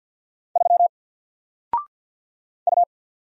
Morse code, heard as a keyed tone, about 53 wpm, 700 Hz. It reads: 3 U
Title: Single Letter-Number 50wpm